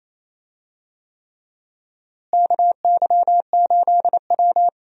Morse code, heard as a keyed tone, about 28 wpm, 700 Hz. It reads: KY8W
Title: USA Call Signs - 4 Characters - Repeated 3x - 28wpm